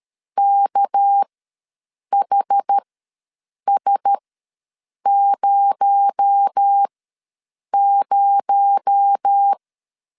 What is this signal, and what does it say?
Morse code, heard as a keyed tone, about 13 wpm, spaced as for 7 wpm, 800 Hz. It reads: KHS00
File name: LEZIONI / 6 PKX0 / 6 PKX0 ws 12wpm 3448